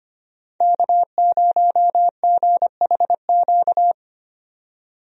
Morse code, heard as a keyed tone, about 25 wpm, 690 Hz. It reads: K0GHQ
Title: USA Call Signs - 5 Characters - 25wpm